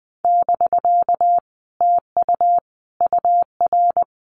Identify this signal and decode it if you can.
Morse code, heard as a keyed tone, about 20 wpm, 700 Hz. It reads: <BK> TU UR
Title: Common QSO Phrases Encore 20wpm